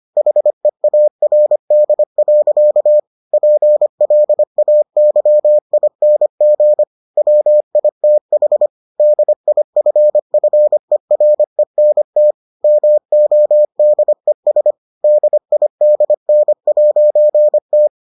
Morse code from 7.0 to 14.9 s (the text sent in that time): WITH DIFFERENT MODES